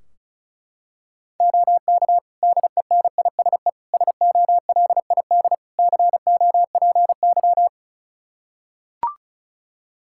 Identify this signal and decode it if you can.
Morse code, heard as a keyed tone, about 36 wpm, 700 Hz. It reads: OK DENISE SOLID COPY
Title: K1USN SST QSO Practice 35wpm